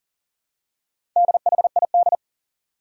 Morse code, heard as a keyed tone, about 40 wpm, 700 Hz. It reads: DHID
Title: Four Challenging Letter-Number Combinations 40wpm